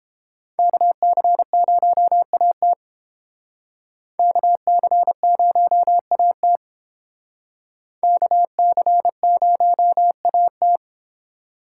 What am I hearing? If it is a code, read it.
Morse code, heard as a keyed tone, about 29 wpm, 700 Hz. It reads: KC0AT KC0AT KC0AT